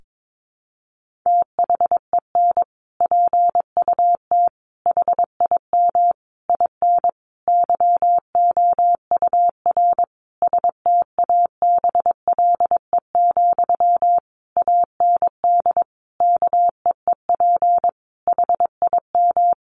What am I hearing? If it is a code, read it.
Morse code, heard as a keyed tone, about 22 wpm, 700 Hz. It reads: THEN PUT HIM IN YOUR STABLE, AND KEEP HIM
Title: Sentences from Top 1500 Words 22wpm